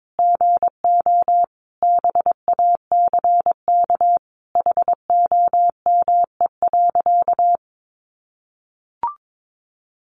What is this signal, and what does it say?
Morse code, heard as a keyed tone, about 22 wpm, 700 Hz. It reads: GO BACK HOME.
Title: Sentences from Top 500 Words 22wpm